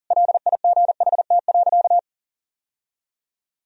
Morse code, heard as a keyed tone, about 40 wpm, 700 Hz. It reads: LIGHT.
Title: Sentences from Top 1000 Words - Review Part 1 - 40wpm